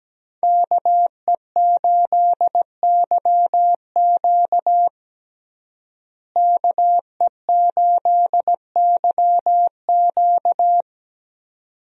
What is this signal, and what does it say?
Morse code, heard as a keyed tone, about 17 wpm, 700 Hz. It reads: KE8YQ KE8YQ
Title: USA Call Signs - 5 Characters - Repeated 2x - 17wpm